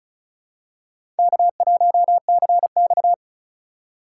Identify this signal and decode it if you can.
Morse code, heard as a keyed tone, about 34 wpm, 700 Hz. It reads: K1CX
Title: K1USN SST QSO Practice - 35wpm